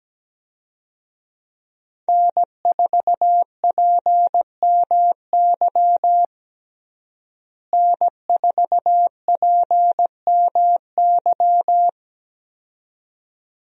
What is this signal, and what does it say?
Morse code, heard as a keyed tone, about 17 wpm, 700 Hz. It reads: N4PMY N4PMY
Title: USA Call Signs - 5 Characters - Repeated 2x - 17wpm